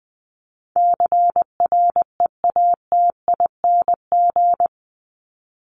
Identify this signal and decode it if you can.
Morse code, heard as a keyed tone, about 20 wpm, 700 Hz. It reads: CREATING
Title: ING Suffix - Top 100 Words with Base Words 20wpm